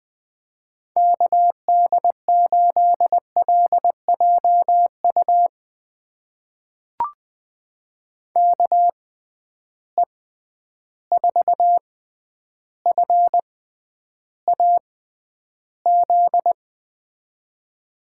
Morse code, heard as a keyed tone, about 20 wpm, spaced as for 10 wpm, 700 Hz. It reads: KD8LJU KE4FAZ
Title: USA Call Signs - 6 Characters - 6x Character Spacing - 20wpm